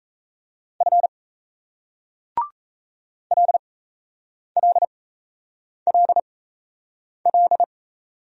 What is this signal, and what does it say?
Morse code, heard as a keyed tone, about 38 wpm, spaced as for 8 wpm, 700 Hz. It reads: F LLLL